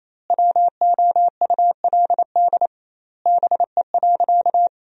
Morse code, heard as a keyed tone, about 28 wpm, 700 Hz. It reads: WOULD BE.
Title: 2x Word-Spacing - Sentences from Top 200 Words - 28wpm